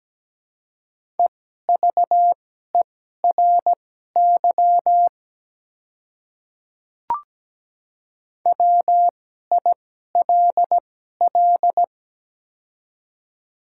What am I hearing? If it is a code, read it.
Morse code, heard as a keyed tone, about 17 wpm, spaced as for 12 wpm, 700 Hz. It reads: EVERY WILL
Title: 2x Character Spacing - Top 200 Words - 17wpm